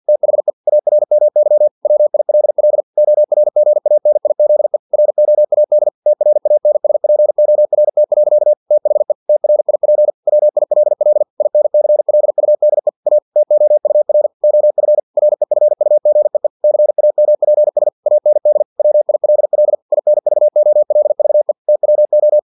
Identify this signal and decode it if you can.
Morse code, heard as a keyed tone, about 50 wpm, 600 Hz. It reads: THE ARMY WILL ORGANIZE ROAD TRANSPORT. THE TRIP WILL INCLUDE A TOUR OF REFUGEE CAMPS AND WILL INVOLVE TWO
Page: Never 49wpm